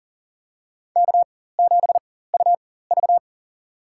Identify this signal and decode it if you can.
Morse code, heard as a keyed tone, about 40 wpm, spaced as for 19 wpm, 700 Hz. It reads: K7UV
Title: USA Call Signs - 4 Characters - 4x Character Spacing - 40wpm